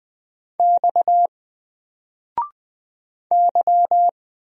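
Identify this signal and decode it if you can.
Morse code, heard as a keyed tone, about 20 wpm, 700 Hz. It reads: X Y
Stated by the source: Single letter and number Morse drill at 20 wpm, from N4LCD.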